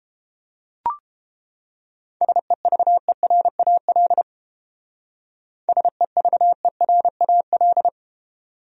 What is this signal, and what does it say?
Morse code, heard as a keyed tone, about 30 wpm, 700 Hz. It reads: SEVERAL SEVERAL